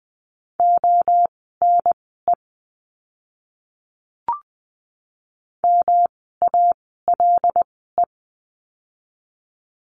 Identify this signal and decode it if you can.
Morse code, heard as a keyed tone, about 20 wpm, spaced as for 15 wpm, 700 Hz. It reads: ONE MALE
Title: 108 Lesson 18f - Letter m - Words 20wpm